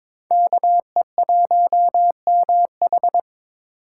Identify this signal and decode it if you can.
Morse code, heard as a keyed tone, about 26 wpm, 700 Hz. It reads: KE1MH
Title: Speed Racing - USA 2x2 Call Signs 22wpm